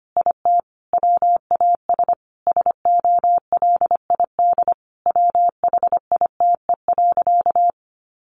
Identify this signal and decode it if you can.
Morse code, heard as a keyed tone, about 25 wpm, 700 Hz. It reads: IT WAS SOLID WHITE.
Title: Sentences from Top 1500 Words 25wpm